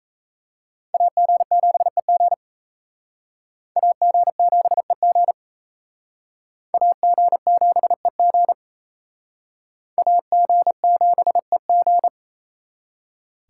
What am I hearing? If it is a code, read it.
Morse code, heard as a keyed tone, about 35 wpm, 700 Hz. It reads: AG7EG AG7EG AG7EG AG7EG
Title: Speed Racing - USA 2x2 Call Signs 28wpm